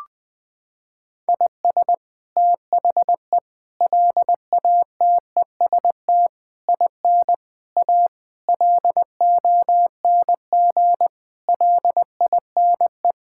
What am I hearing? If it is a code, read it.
Morse code, heard as a keyed tone, about 21 wpm, 700 Hz. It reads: IS THE LATEST IN A LONG LINE
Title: Sets of 8 Words - 20wpm